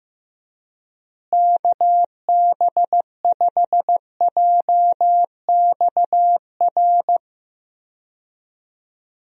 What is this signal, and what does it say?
Morse code, heard as a keyed tone, about 15 wpm, 700 Hz. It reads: KB5JXR
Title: USA Call Signs - 6 Characters - Repeated 3x - 15wpm